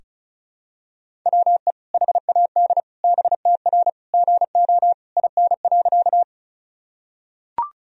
Morse code, heard as a keyed tone, about 36 wpm, 700 Hz. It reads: WE HAD BTR GO IN.
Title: Sentences from Top 400 Words and Abbreviations 35wpm